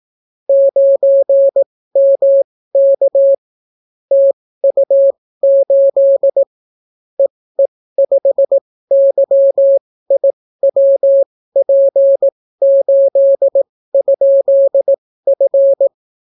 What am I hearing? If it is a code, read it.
Morse code, heard as a keyed tone, about 18 wpm, spaced as for 14 wpm, 550 Hz. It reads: 9MK TU8 EE5YIWP8?F